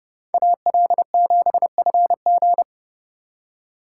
Morse code, heard as a keyed tone, about 37 wpm, 700 Hz. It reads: AL7FG
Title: Speed Racing - USA 2x2 Call Signs 30wpm